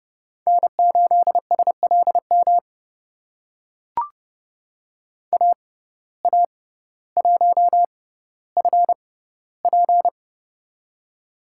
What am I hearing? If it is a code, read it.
Morse code, heard as a keyed tone, about 30 wpm, spaced as for 15 wpm, 690 Hz. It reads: N8SLM AA1FP
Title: USA Call Signs - 5 Characters - 6x Character Spacing - 30wpm